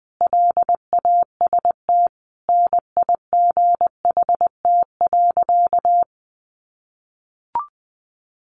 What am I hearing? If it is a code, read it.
Morse code, heard as a keyed tone, about 20 wpm, 700 Hz. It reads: LAST NIGHT.
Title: Sentences from Top 1300 Words 20wpm